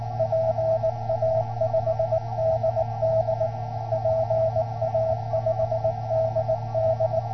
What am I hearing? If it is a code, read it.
Morse code, heard as a keyed tone, about 19 wpm, 650 Hz. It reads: PA5DD PA5DD